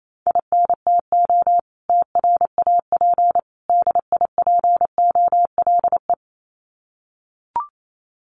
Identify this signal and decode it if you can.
Morse code, heard as a keyed tone, about 28 wpm, 700 Hz. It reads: INTO TRAP DIPOLE
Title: Common QSO Phrases 28wpm